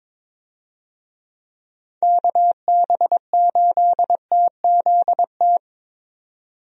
Morse code, heard as a keyed tone, about 21 wpm, 700 Hz. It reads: KB8TZT
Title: USA Call Signs - 6 Characters - 22wpm